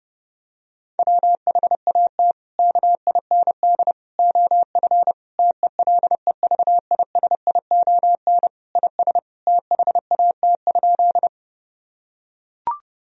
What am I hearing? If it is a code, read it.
Morse code, heard as a keyed tone, about 32 wpm, 700 Hz. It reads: WHAT KIND OF TELEVISION IS THAT?
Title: Sentences from Top 1000 Words - Review Part 4 - 30wpm